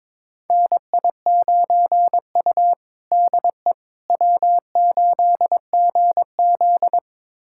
Morse code, heard as a keyed tone, about 22 wpm, 700 Hz. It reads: NI9U DE W8GZ
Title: USA Callsign DE Callsign - 4 Characters - 22wpm